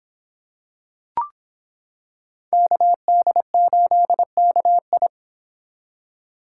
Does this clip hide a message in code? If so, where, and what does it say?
Yes: Morse code from 2.5 to 5.1 s, heard as a keyed tone, about 26 wpm, 700 Hz. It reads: KD8KI